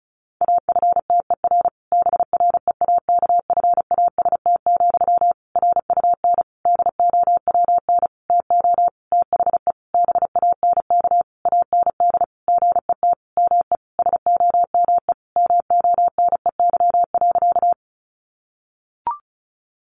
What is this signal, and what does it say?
Morse code, heard as a keyed tone, about 34 wpm, 700 Hz. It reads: AFTER BREAKFAST, RUN DOWN TO THE BANK AND GET ME SOME MONEY.
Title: Sentences from Top 1500 Words 35wpm